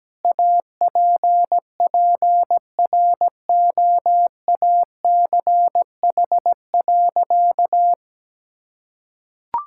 Morse code, heard as a keyed tone, about 17 wpm, 700 Hz. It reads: APPROACH.
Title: Sentences from Top 1000 Words - Review Part 6 - 17wpm